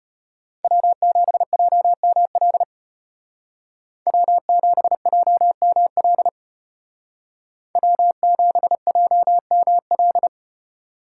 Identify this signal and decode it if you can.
Morse code, heard as a keyed tone, about 32 wpm, 700 Hz. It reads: W7JML W7JML W7JML